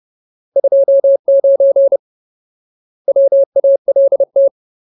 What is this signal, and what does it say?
Morse code, heard as a keyed tone, about 30 wpm, 550 Hz. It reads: 29 WALT
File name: Ultimate Word List / 30WPM Ultimate